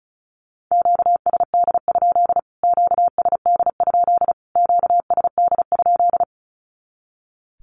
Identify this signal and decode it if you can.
Morse code, heard as a keyed tone, about 34 wpm, 700 Hz. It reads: QSD? QSD? QSD?
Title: Q-Codes Repeated 3x 35wpm